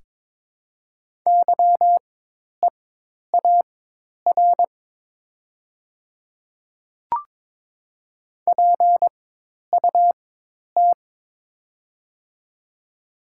Morse code, heard as a keyed tone, about 22 wpm, spaced as for 10 wpm, 690 Hz. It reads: YEAR PUT
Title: 4x Character Spacing - Top 300 Words - 22wpm